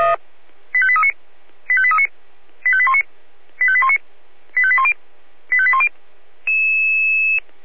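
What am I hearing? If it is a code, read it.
Five-tone selective call (ZVEI), heard as quick runs of five tones, tones of about 70 ms. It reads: 85839 85839 85819 85819 85810 85810
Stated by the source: ZVEI five-tone selective call received by SDR.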